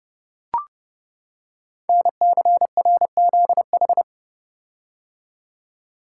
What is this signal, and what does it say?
Morse code, heard as a keyed tone, about 30 wpm, 700 Hz. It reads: NCRZH
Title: Code Groups - 5 Letters - 30wpm